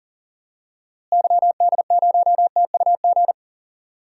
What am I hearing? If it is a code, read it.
Morse code, heard as a keyed tone, about 38 wpm, 700 Hz. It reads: YD0TUG
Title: DXCC-100 Call Signs - 6 Characters - Repeated 2x - 40wpm